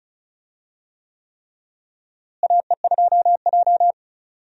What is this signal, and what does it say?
Morse code, heard as a keyed tone, about 36 wpm, 700 Hz. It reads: AE2J
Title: USA Call Signs - 4 Characters - Repeated 3x - 35wpm